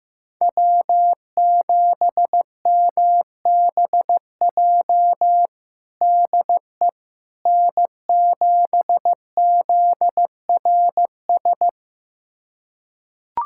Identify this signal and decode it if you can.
Morse code, heard as a keyed tone, about 15 wpm, 700 Hz. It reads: W7MBJ DE N7ZRS
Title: USA Callsign DE Callsign - 5 Characters - 15wpm